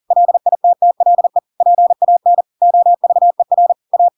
Morse code, heard as a keyed tone, about 40 wpm, 700 Hz. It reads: LITTLE PAN OVER A
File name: Pinocchio / Pinocchio ch5 40 40 ik1hge